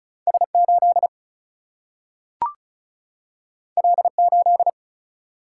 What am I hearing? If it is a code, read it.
Morse code, heard as a keyed tone, about 36 wpm, 700 Hz. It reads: S 8 L 8
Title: Single-Letter Single-Number 35wpm